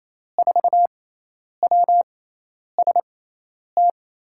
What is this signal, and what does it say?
Morse code, heard as a keyed tone, about 28 wpm, spaced as for 10 wpm, 700 Hz. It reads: 4WST
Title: USA Call Signs - 6 Characters - 6x Character Spacing - 28wpm